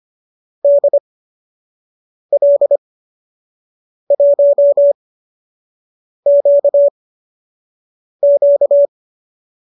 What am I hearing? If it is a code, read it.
Morse code, heard as a keyed tone, about 25 wpm, spaced as for 6 wpm, 590 Hz. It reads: DL1QQ
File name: qso012 06